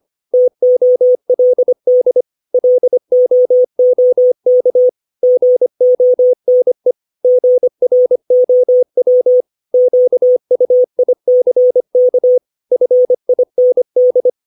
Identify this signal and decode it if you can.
Morse code, heard as a keyed tone, about 25 wpm, 500 Hz. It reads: TOLD LOOK GONE GROW QUICK FIND
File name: 632 words / 632 Words 25WPM